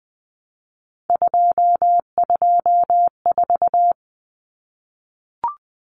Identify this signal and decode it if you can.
Morse code, heard as a keyed tone, about 20 wpm, 700 Hz. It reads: 224
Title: Common Cut Numbers - 3-Digit 20wpm